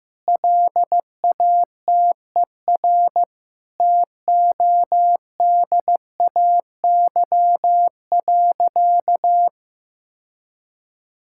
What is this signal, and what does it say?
Morse code, heard as a keyed tone, about 15 wpm, 700 Hz. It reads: LATER TODAY.
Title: Sentences from Top 1000 Words - Review Part 6 - 15wpm